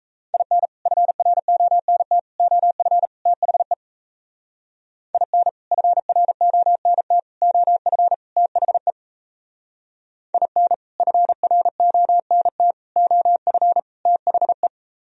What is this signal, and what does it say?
Morse code, heard as a keyed tone, about 36 wpm, 700 Hz. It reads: IN FRONT OF THE IN FRONT OF THE IN FRONT OF THE